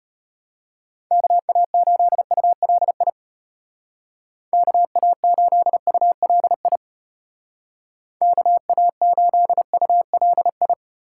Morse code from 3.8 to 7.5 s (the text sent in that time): KA8ULI